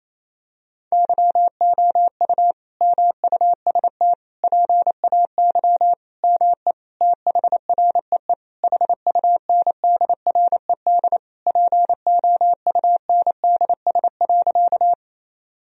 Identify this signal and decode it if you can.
Morse code, heard as a keyed tone, about 28 wpm, 700 Hz. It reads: YOU MUST PAY ME THREE HUNDRED POUNDS.
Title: Sentences from Top 1000 Words - Review Part 2 - 28wpm